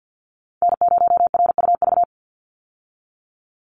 Morse code, heard as a keyed tone, about 50 wpm, 700 Hz. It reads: N0RUV
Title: USA Call Signs 50wpm